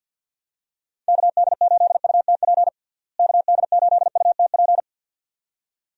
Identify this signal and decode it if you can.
Morse code, heard as a keyed tone, about 48 wpm, 700 Hz. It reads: KD8UTP KD8UTP
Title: USA Call Signs - 6 Characters - Repeated 2x - 50wpm